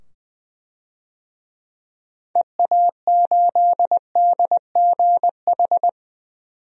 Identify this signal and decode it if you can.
Morse code, heard as a keyed tone, about 20 wpm, 700 Hz. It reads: EA8DGH
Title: DXCC-100 Call Signs - 6 Characters - 20wpm